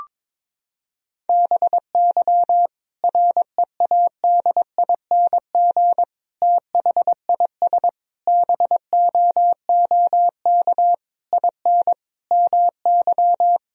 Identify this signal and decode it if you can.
Morse code, heard as a keyed tone, about 22 wpm, 700 Hz. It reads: BY READING THIS BOOK IN MY
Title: Sentences from Top 1000 Words - Review Part 3 - 22wpm